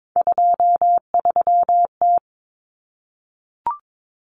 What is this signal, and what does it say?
Morse code, heard as a keyed tone, about 22 wpm, 700 Hz. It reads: 23T
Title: Common Cut Numbers - 3-Digit 22wpm